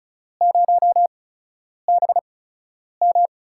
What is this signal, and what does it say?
Morse code, heard as a keyed tone, about 36 wpm, spaced as for 10 wpm, 700 Hz. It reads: 0BM
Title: USA Call Signs - 8x Character Spacing - 35wpm